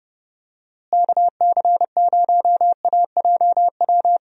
Morse code, heard as a keyed tone, about 30 wpm, 700 Hz. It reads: KC0AJW
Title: USA Call Signs - 6 Characters - Repeated 3x - 30wpm